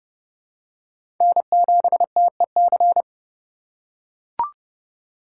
Morse code, heard as a keyed tone, about 30 wpm, 700 Hz. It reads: N7TEC
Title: USA Call Signs - 5 Characters - 3x Character Spacing - 30wpm